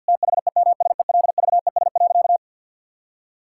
Morse code, heard as a keyed tone, about 50 wpm, 700 Hz. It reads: THEMSELVES.
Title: Sentences from Top 1000 Words - Review Part 5 - 50wpm